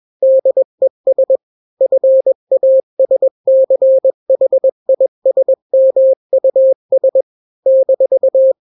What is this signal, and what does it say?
Morse code, heard as a keyed tone, about 21 wpm, 530 Hz. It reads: DES FASCHISMUS -
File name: daily / 20wpm